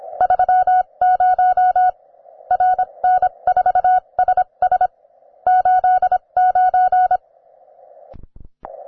Morse code, heard as a keyed tone, about 26 wpm, spaced as for 22 wpm, 740 Hz. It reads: 30 RN4SS 89